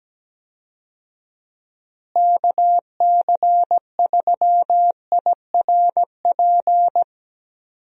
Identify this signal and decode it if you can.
Morse code, heard as a keyed tone, about 17 wpm, 700 Hz. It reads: KC3IRP
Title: USA Call Signs - 6 Characters - Repeated 3x - 17wpm